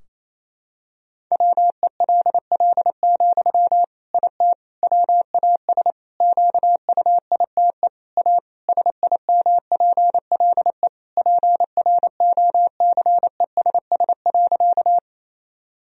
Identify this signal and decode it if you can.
Morse code, heard as a keyed tone, about 28 wpm, 700 Hz. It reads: WELL, IT WAS QUITE A SIMPLE PROCESS.
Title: Sentences from Top 1000 Words - Review Part 3 - 28wpm